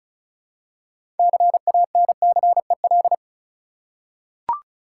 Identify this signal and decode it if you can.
Morse code, heard as a keyed tone, about 36 wpm, 700 Hz. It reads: CANCEL
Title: Six Letter Words - 35wpm